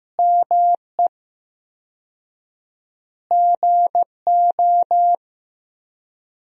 Morse code, heard as a keyed tone, about 15 wpm, 700 Hz. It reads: ME GO
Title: 4x Word-Spacing - Sentences from Top 300 Words - 15wpm